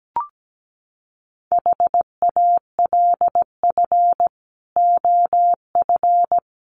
Morse code, heard as a keyed tone, about 17 wpm, 700 Hz. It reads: HALF OF
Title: Sets of 3 Words 17wpm